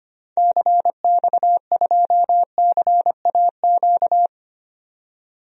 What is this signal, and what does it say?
Morse code, heard as a keyed tone, about 25 wpm, 700 Hz. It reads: CX2CAQ
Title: DXCC-100 Call Signs - 6 Characters - 25wpm